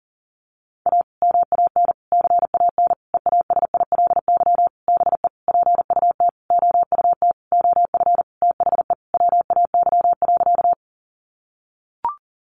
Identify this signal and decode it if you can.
Morse code, heard as a keyed tone, about 40 wpm, 700 Hz. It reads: A MAN CAN EASILY BE PUT OUT OF THE WAY.
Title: Sentences from Top 900 Words 40wpm